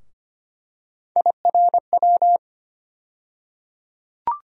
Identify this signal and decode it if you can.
Morse code, heard as a keyed tone, about 25 wpm, 700 Hz. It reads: IRW
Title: USA Call Sign Postfixes - 3 Letters 25wpm